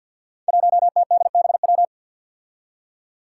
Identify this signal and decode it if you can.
Morse code, heard as a keyed tone, about 53 wpm, 700 Hz. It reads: 1TDBW
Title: Five Challenging Letter-Number Combinations 50wpm